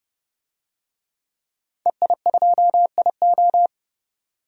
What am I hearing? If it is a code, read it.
Morse code, heard as a keyed tone, about 30 wpm, 700 Hz. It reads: EI2IO
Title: DXCC-100 Call Signs - 5 Characters - 30wpm